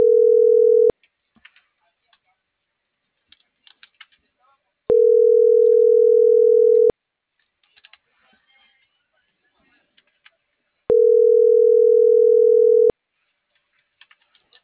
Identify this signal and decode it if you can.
Ringback tone (North America), 440 + 480 Hz.